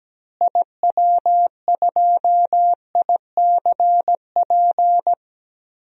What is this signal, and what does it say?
Morse code, heard as a keyed tone, about 17 wpm, 700 Hz. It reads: IW2ICP
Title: DXCC-100 Call Signs - 6 Characters - Repeated 2x - 17wpm